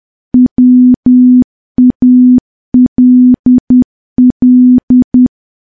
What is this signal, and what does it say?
Morse code, heard as a keyed tone, about 10 wpm, 250 Hz. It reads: WALL
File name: morse / morse-training-10WPM